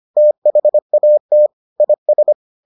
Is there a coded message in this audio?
Yes: Morse code from 0.2 to 2.3 s, heard as a keyed tone, about 25 wpm, 600 Hz. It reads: THAT IS